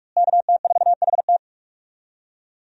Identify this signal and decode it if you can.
Morse code, heard as a keyed tone, about 46 wpm, 700 Hz. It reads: KT4HT
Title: USA Call Signs - 5 Characters - Repeated 2x - 45wpm